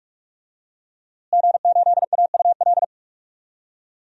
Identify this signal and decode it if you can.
Morse code, heard as a keyed tone, about 46 wpm, 700 Hz. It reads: G8AUL
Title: DXCC-100 Call Signs - 5 Characters - Repeated 2x - 45wpm